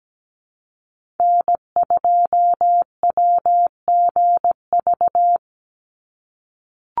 Morse code, heard as a keyed tone, about 17 wpm, 700 Hz. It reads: N2WGV
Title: USA Call Signs 17wpm